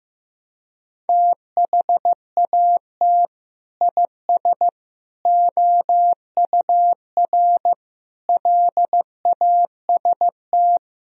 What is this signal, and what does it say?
Morse code, heard as a keyed tone, about 15 wpm, 700 Hz. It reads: THAT IS OUR LAST
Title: Sentences from Top 1000 Words - Review Part 4 - 15wpm